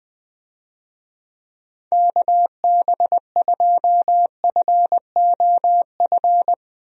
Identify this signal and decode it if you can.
Morse code, heard as a keyed tone, about 21 wpm, 700 Hz. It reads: KB2FOF
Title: USA Call Signs - 6 Characters - Repeated 2x - 20wpm